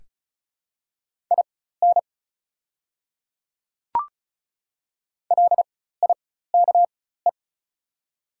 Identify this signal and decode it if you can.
Morse code, heard as a keyed tone, about 34 wpm, spaced as for 16 wpm, 700 Hz. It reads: IN LIKE
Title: 4x Character Spacing - Top 100 Words - 35wpm